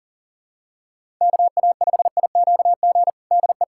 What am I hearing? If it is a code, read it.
Morse code, heard as a keyed tone, about 40 wpm, 700 Hz. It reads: KA5IQG DE
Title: USA Callsign DE Callsign - 6 Characters - 40wpm